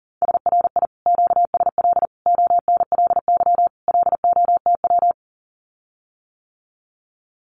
Morse code, heard as a keyed tone, about 40 wpm, 700 Hz. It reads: SRI QSL ONLY LOTW
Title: Common QSO Phrases Encore 40wpm